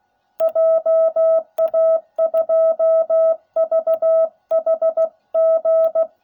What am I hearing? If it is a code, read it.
Morse code, heard as a keyed tone, about 16 wpm, 660 Hz. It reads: JA2VHG